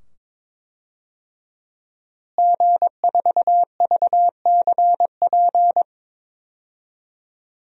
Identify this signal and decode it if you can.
Morse code, heard as a keyed tone, about 22 wpm, 700 Hz. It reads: G4VCP
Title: DXCC-100 Call Signs - 5 Characters - Repeated 2x - 22wpm